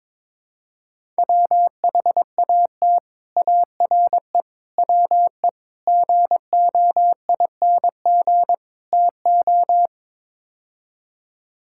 Morse code, heard as a keyed tone, about 22 wpm, 700 Hz. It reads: WHAT ARE WE GOING TO
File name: practice / 22wpm / Build A Phrase - 5-Word Phrases - 22wpm